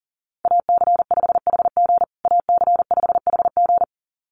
Morse code, heard as a keyed tone, about 40 wpm, 700 Hz. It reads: AC5HG AC5HG
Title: USA Call Signs Repeated 2x 40wpm